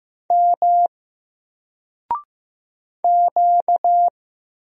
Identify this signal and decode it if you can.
Morse code, heard as a keyed tone, about 15 wpm, 700 Hz. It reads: M Q
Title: Single Letter-Number 15wpm